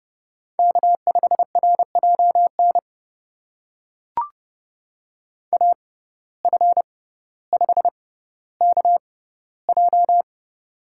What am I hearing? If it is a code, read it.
Morse code, heard as a keyed tone, about 30 wpm, spaced as for 15 wpm, 700 Hz. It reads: K5RJN AF5KJ
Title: USA Call Signs - 5 Characters - 6x Character Spacing - 30wpm